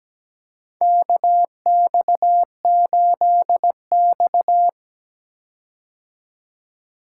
Morse code, heard as a keyed tone, about 17 wpm, 700 Hz. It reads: KX8X
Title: K1USN SST QSO Practice - 17wpm